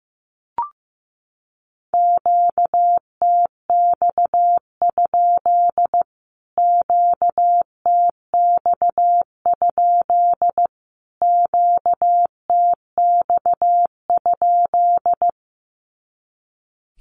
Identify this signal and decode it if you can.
Morse code, heard as a keyed tone, about 15 wpm, 700 Hz. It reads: QTX? QTX? QTX?
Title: Q-Codes Repeated 3x 15wpm